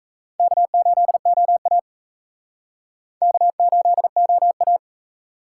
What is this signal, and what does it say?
Morse code, heard as a keyed tone, about 36 wpm, 700 Hz. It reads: K8OA K8OA